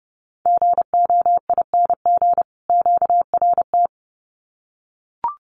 Morse code, heard as a keyed tone, about 30 wpm, 700 Hz. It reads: GOING QRT
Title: Common QSO Phrases 30wpm